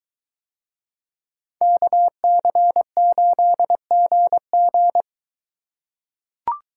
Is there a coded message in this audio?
Yes: Morse code from 1.6 to 5.0 s, heard as a keyed tone, about 22 wpm, 700 Hz. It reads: KC8GG